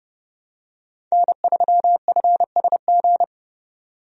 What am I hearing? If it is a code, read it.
Morse code, heard as a keyed tone, about 31 wpm, 700 Hz. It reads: N3FSG
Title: USA Call Signs - 5 Characters - 30wpm